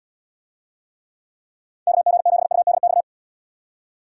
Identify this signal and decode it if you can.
Morse code, heard as a keyed tone, about 74 wpm, 700 Hz. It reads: KG7UDX